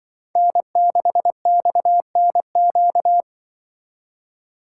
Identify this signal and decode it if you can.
Morse code, heard as a keyed tone, about 24 wpm, 700 Hz. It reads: N6XNQ
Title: Speed Racing - USA Call Signs 20wpm